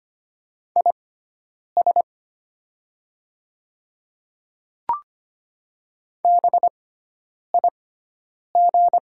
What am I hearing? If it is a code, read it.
Morse code, heard as a keyed tone, about 25 wpm, spaced as for 9 wpm, 700 Hz. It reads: IS BIG
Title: 6x Character Spacing - Top 200 Words - 25wpm